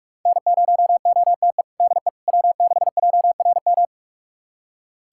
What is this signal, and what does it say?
Morse code, heard as a keyed tone, about 41 wpm, 700 Hz. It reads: N0OTE DE W6JRM
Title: USA Callsign DE Callsign - 5 Characters - 45wpm